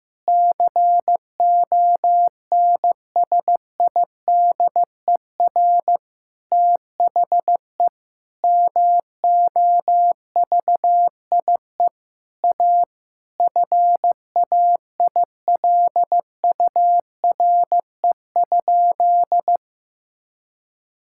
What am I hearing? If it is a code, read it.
Morse code, heard as a keyed tone, about 15 wpm, 700 Hz. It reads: CONSIDER THE MOVIE A FAILURE?
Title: Sentences from Top 1000 Words - Review Part 6 - 15wpm